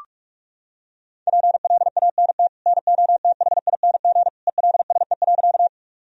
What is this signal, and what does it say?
Morse code, heard as a keyed tone, about 42 wpm, 700 Hz. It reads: PLANT NOTHING ELSE.
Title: Sentences from Top 1000 Words - Review Part 1 - 45wpm